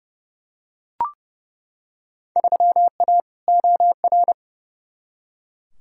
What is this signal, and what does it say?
Morse code, heard as a keyed tone, about 29 wpm, 700 Hz. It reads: 3A OR
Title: Field Day Exchanges Encore - 30wpm